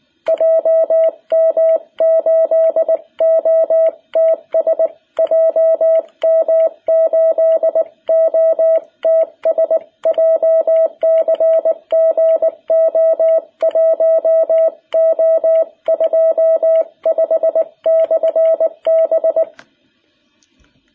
Morse code, heard as a keyed tone, about 19 wpm, 660 Hz. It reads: JM8OTS JM8OTS JCG O1O25/B
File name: jh8cbh / 2021cw